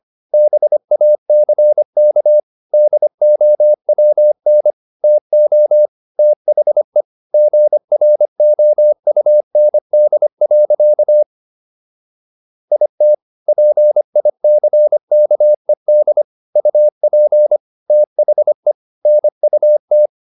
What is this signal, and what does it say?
Morse code, heard as a keyed tone, about 25 wpm, 600 Hz. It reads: BACK DOWN TO THE GROUND. IT PICKED UP THE NUT